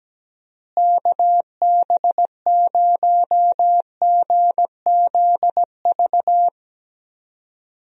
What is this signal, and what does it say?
Morse code, heard as a keyed tone, about 17 wpm, 700 Hz. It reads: KB0GZV
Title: USA Call Signs - 6 Characters - Repeated 3x - 17wpm